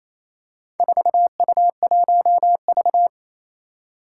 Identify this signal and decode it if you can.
Morse code, heard as a keyed tone, about 27 wpm, 700 Hz. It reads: 4U1V
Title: Four Challenging Letter-Number Combinations 28wpm